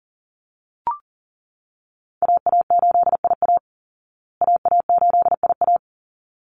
Morse code, heard as a keyed tone, about 40 wpm, 700 Hz. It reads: AA8IA AA8IA
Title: Call Signs Repeated 3x - 40wpm